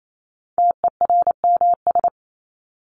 Morse code, heard as a keyed tone, about 28 wpm, 700 Hz. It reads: TERMS
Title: ES:S Suffix - Top 100 Words with Base Words 28wpm